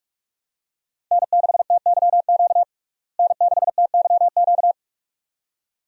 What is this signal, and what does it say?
Morse code, heard as a keyed tone, about 42 wpm, 700 Hz. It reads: N6TYQ N6TYQ
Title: USA Call Signs - 5 Characters - Repeated 2x - 45wpm